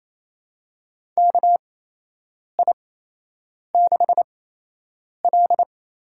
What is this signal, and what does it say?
Morse code, heard as a keyed tone, about 28 wpm, spaced as for 8 wpm, 700 Hz. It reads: KI6L
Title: USA Call Signs - 8x Character Spacing - 28wpm